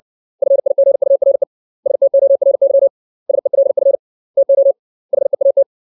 Morse code, heard as a keyed tone, about 60 wpm, 550 Hz. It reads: FIGURE STORY SELF TO HEAT
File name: 632 words / 632 Words 60WPM